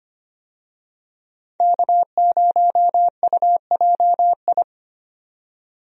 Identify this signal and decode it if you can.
Morse code, heard as a keyed tone, about 25 wpm, 700 Hz. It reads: K0UJI
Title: USA Call Signs - 5 Characters - 25wpm